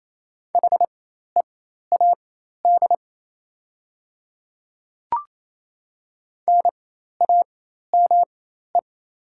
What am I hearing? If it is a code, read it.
Morse code, heard as a keyed tone, about 28 wpm, spaced as for 13 wpm, 700 Hz. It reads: HEAD NAME
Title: 4x Character Spacing - Top 300 Words - 28wpm